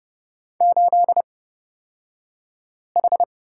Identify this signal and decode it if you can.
Morse code, heard as a keyed tone, about 30 wpm, spaced as for 5 wpm, 700 Hz. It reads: 8H
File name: Lesson 37d - Letter x - Two Characters 30wpm